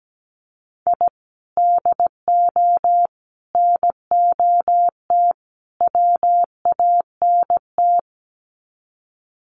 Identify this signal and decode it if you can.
Morse code, heard as a keyed tone, about 17 wpm, 700 Hz. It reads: I DO NOT WANT
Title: Sets of 4 Words 17wpm